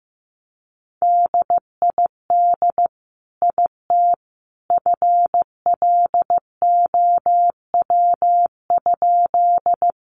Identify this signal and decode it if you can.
Morse code, heard as a keyed tone, about 15 wpm, 700 Hz. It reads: DID IT FLOW?
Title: Sentences from Top 1200 Words 15wpm